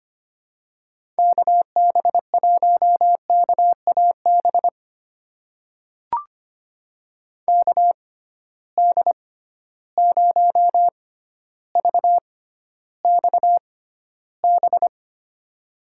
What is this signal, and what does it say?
Morse code, heard as a keyed tone, about 25 wpm, spaced as for 13 wpm, 700 Hz. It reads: KB1KAB KD0VXB
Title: USA Call Signs - 6 Characters - 6x Character Spacing - 25wpm